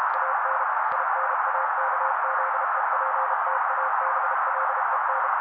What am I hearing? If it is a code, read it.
Morse code, heard as a keyed tone, about 32 wpm, 530 Hz. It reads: AN INAKZEPTABLEN
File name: daily / 31wpm noise